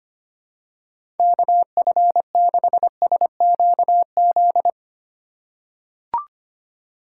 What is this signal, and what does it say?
Morse code, heard as a keyed tone, about 25 wpm, 700 Hz. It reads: KF6SQZ
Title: USA Call Signs - 6 Characters - 6x Character Spacing - 25wpm